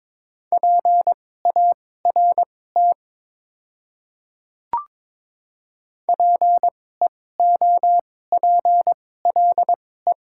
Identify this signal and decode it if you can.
Morse code, heard as a keyed tone, about 22 wpm, spaced as for 16 wpm, 700 Hz. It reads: PART PEOPLE
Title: 2x Character Spacing - Top 200 Words - 22wpm